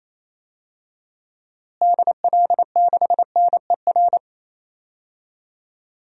Morse code, heard as a keyed tone, about 28 wpm, 700 Hz. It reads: DL6NER
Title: DXCC-100 Call Signs - 6 Characters - Repeated 3x - 28wpm